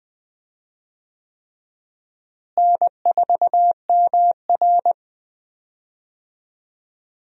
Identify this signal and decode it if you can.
Morse code, heard as a keyed tone, about 20 wpm, 700 Hz. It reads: N4MR
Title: USA Call Signs - 4 Characters - Repeated 3x - 20wpm